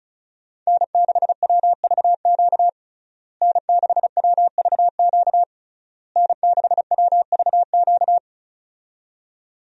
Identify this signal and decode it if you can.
Morse code, heard as a keyed tone, about 36 wpm, 700 Hz. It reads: N6WVQ N6WVQ N6WVQ
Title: USA Call Signs - 5 Characters - Repeated 3x - 35wpm